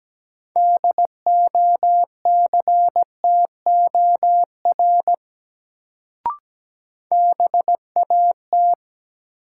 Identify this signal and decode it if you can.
Morse code, heard as a keyed tone, about 17 wpm, 700 Hz. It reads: DOCTOR BAT
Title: Sentences from Top 1500 Words - New Word Drill Alt - 17wpm